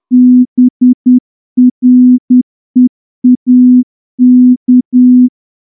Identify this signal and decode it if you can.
Morse code, heard as a keyed tone, about 9 wpm, 250 Hz. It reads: BREAK
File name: morse-training-10WPM